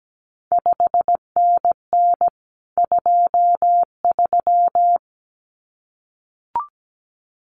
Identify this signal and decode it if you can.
Morse code, heard as a keyed tone, about 17 wpm, 700 Hz. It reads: 5NN 23
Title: CQ WW DX Contest Exchange Encore - 17wpm